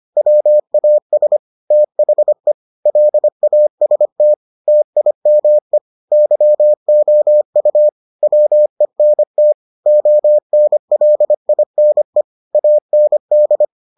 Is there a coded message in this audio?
Yes: Morse code from 0.2 to 13.6 s, heard as a keyed tone, about 25 wpm, 600 Hz. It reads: WAS THE LAST TIME YOU WENT ONLINE AND